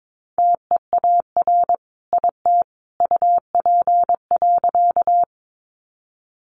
Morse code, heard as a keyed tone, about 22 wpm, 700 Hz. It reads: TEAR IT UP.
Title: A Sound Of Thunder 22wpm